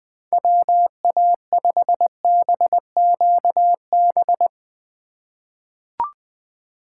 Morse code, heard as a keyed tone, about 21 wpm, 700 Hz. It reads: WA5BQB
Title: USA Call Signs - 6 Characters - 4x Character Spacing - 20wpm